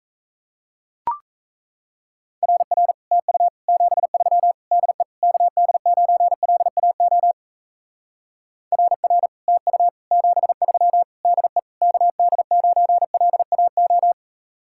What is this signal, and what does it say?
Morse code, heard as a keyed tone, about 34 wpm, 700 Hz. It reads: RR TU 73 DE KD9LAO RR TU 73 DE KD9LAO
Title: Speed-Racing Common QSO Phrases - 28wpm